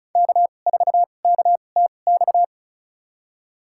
Morse code, heard as a keyed tone, about 36 wpm, spaced as for 25 wpm, 700 Hz. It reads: K4KTX
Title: USA Call Signs - 5 Characters - 2x Character Spacing - 35wpm